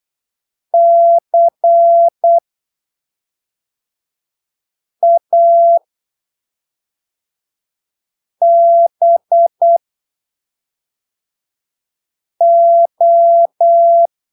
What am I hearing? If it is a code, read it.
Morse code, heard as a keyed tone, about 8 wpm, spaced as for 3 wpm, 680 Hz. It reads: CABO
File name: arquivos / texto / texto 16 08ppm